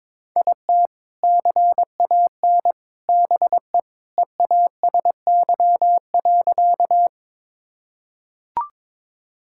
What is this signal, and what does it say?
Morse code, heard as a keyed tone, about 22 wpm, 700 Hz. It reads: IT CAN BE EASY.
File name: practice / 22wpm / Sentences with 2-Word Phrases - 22wpm